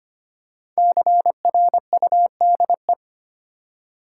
Morse code, heard as a keyed tone, about 25 wpm, 700 Hz. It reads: CRUDE
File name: 5 Letter Words - First 3 Letters Given - 25wpm